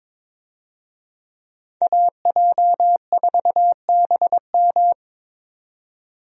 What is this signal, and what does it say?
Morse code, heard as a keyed tone, about 21 wpm, 700 Hz. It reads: AJ4BM
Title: USA Call Signs - 5 Characters - 22wpm